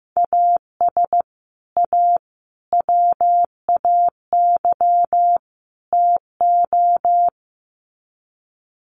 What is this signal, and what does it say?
Morse code, heard as a keyed tone, about 15 wpm, 700 Hz. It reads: AS A WAY TO
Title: Sets of 4 Words 15wpm